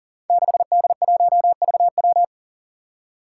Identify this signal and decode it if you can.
Morse code, heard as a keyed tone, about 38 wpm, 700 Hz. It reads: 6D1VW
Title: Five Challenging Letter-Number Combinations 40wpm